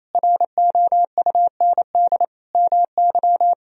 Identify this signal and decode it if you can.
Morse code, heard as a keyed tone, about 27 wpm, 700 Hz. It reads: ROUND MY
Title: Sentences from Top 200 Words and Abbreviations 28wpm